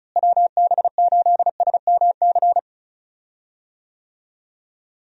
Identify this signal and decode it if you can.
Morse code, heard as a keyed tone, about 36 wpm, 700 Hz. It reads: WB8SMC
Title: USA Call Signs - 6 Characters - Repeated 2x - 35wpm